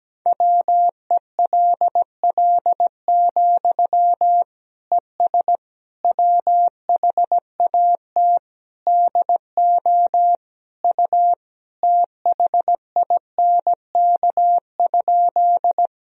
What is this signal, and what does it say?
Morse code, heard as a keyed tone, about 17 wpm, 700 Hz. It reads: WELL, ES WHAT DO U THINK?
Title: Sentences from Top 200 Words and Abbreviations 17wpm